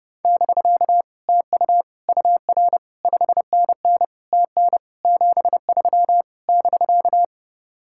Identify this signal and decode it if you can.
Morse code, heard as a keyed tone, about 30 wpm, 700 Hz. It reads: <BK> TU UR 5NN TN 73 <BK>